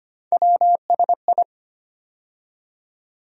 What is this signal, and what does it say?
Morse code, heard as a keyed tone, about 25 wpm, 700 Hz. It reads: WSI
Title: USA Call Sign Postfixes - 3 Letters 25wpm